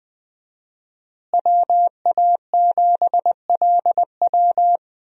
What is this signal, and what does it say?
Morse code, heard as a keyed tone, about 21 wpm, 700 Hz. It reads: WA7LW
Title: USA Call Signs - 5 Characters - Repeated 3x - 20wpm